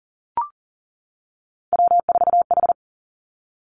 Morse code, heard as a keyed tone, about 40 wpm, 700 Hz. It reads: W4H
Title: Three Challenging Letter-Number Combinations 40wpm